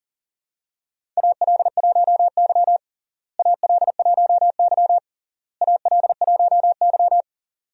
Morse code, heard as a keyed tone, about 40 wpm, 700 Hz. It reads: AL1Y AL1Y AL1Y
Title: USA Call Signs - 4 Characters - Repeated 3x - 40wpm